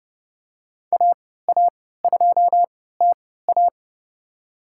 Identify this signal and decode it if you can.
Morse code, heard as a keyed tone, about 30 wpm, spaced as for 17 wpm, 700 Hz. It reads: AA2TA
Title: USA Call Signs - 5 Characters - 3x Character Spacing - 30wpm